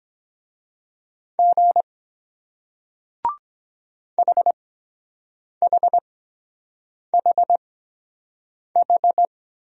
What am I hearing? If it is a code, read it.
Morse code, heard as a keyed tone, about 23 wpm, spaced as for 7 wpm, 700 Hz. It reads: G HHHH